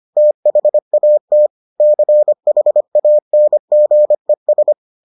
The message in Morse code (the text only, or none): THAT CHANGES